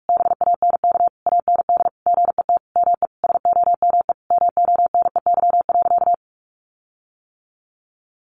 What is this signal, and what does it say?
Morse code, heard as a keyed tone, about 44 wpm, 700 Hz. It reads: BANK AND GET ME SOME MONEY.
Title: Sentences from Top 1500 Words 45wpm